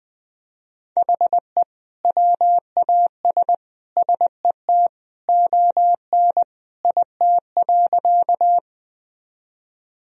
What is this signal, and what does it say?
Morse code, heard as a keyed tone, about 21 wpm, 700 Hz. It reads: HE WAS SET ON IT.
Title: Sentences from Top 1000 Words - Review Part 1 - 20wpm